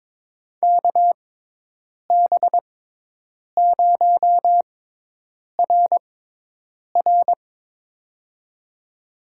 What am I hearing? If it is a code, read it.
Morse code, heard as a keyed tone, about 22 wpm, spaced as for 8 wpm, 700 Hz. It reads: KB0RR
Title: USA Call Signs - 5 Characters - 6x Character Spacing - 22wpm